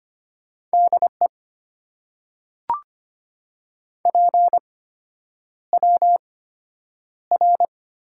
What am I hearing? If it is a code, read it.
Morse code, heard as a keyed tone, about 25 wpm, spaced as for 9 wpm, 690 Hz. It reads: DE PWR
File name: Top 90 QSO Elements - 8x Character Spacing - 25wpm